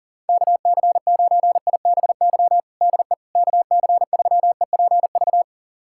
Morse code, heard as a keyed tone, about 40 wpm, 700 Hz. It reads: KC9IBY DE KC3EPV
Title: USA Callsign DE Callsign - 6 Characters - 40wpm